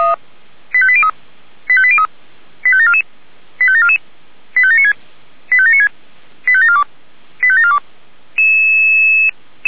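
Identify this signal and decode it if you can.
Five-tone selective call (ZVEI), heard as quick runs of five tones, tones of about 70 ms. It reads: 85803 85803 85744 85744 85796 85796 85742 85742
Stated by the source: Five-tone ZVEI selective call 85719, received by SDR.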